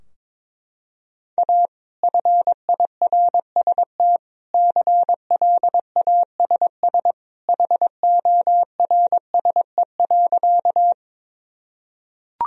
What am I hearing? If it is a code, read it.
Morse code, heard as a keyed tone, about 22 wpm, 700 Hz. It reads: A FIRST CLASS HORSE.
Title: Sentences from Top 1000 Words - Review Part 2 - 22wpm